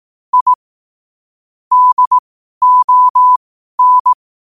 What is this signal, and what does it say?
Morse code, heard as a keyed tone, about 18 wpm, spaced as for 12 wpm, 1000 Hz. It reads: I DON
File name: qotdfiles / 12wpm